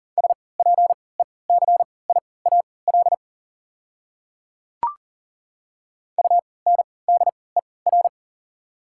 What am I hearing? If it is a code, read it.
Morse code, heard as a keyed tone, about 40 wpm, spaced as for 23 wpm, 690 Hz. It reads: SPECIAL UNDER
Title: 3x Character Spacing - Top 500 Words - 40wpm